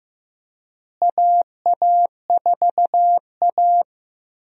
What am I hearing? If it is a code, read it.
Morse code, heard as a keyed tone, about 15 wpm, 700 Hz. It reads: AA4A
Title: USA Call Signs - 4 Characters - Repeated 2x - 15wpm